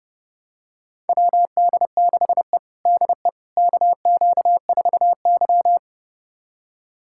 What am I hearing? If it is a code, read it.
Morse code, heard as a keyed tone, about 30 wpm, 700 Hz. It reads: WD6E DE KQ4Y
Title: USA Callsign DE Callsign - 4 Characters - 30wpm